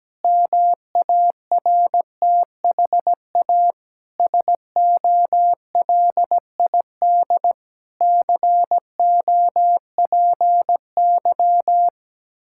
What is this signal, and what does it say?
Morse code, heard as a keyed tone, about 17 wpm, 700 Hz. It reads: MARTHA SOLID COPY
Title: Practice QSO - Exchange 3 - More Info - 17wpm